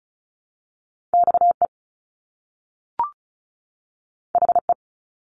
Morse code, heard as a keyed tone, about 36 wpm, 700 Hz. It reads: XE HE
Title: Top Two Letter Combinations in Words 35wpm